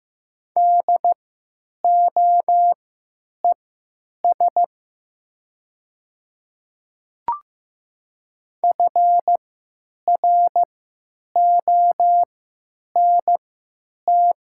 Morse code, heard as a keyed tone, about 15 wpm, spaced as for 9 wpm, 700 Hz. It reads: DOES FRONT
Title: 3x Character Spacing - Top 500 Words - 15wpm